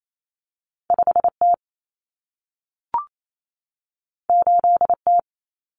Morse code, heard as a keyed tone, about 28 wpm, 700 Hz. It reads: 5T 8T
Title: Common Cut Numbers - 2-Digit 28wpm